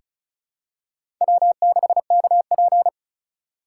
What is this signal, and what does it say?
Morse code, heard as a keyed tone, about 36 wpm, 700 Hz. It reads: W6KP
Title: USA Call Signs - 4 Characters - 2x Character Spacing - 35wpm